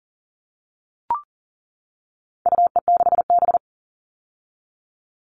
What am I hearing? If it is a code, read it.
Morse code, heard as a keyed tone, about 40 wpm, 700 Hz. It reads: UE6B
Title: All Cut Numbers - 4-Digit 40wpm